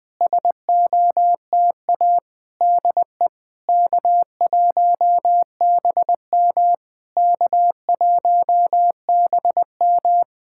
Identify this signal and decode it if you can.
Morse code, heard as a keyed tone, about 20 wpm, 700 Hz. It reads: SOTA DE K1BM K1BM
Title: SOTA QSO Practice 20wpm